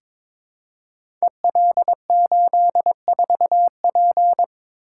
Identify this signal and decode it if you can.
Morse code, heard as a keyed tone, about 22 wpm, 700 Hz. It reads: EL84P
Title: Code Groups - 5 Letter:Numbers - 22wpm